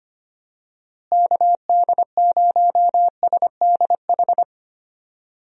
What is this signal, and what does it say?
Morse code, heard as a keyed tone, about 25 wpm, 700 Hz. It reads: KD0SDH
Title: USA Call Signs - 6 Characters - Repeated 3x - 25wpm